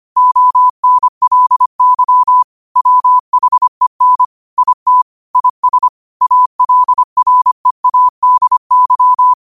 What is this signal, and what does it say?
Morse code, heard as a keyed tone, about 25 wpm, 1000 Hz. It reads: ONLY WHEN IT IS ALREADY
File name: qotdfiles / 25wpm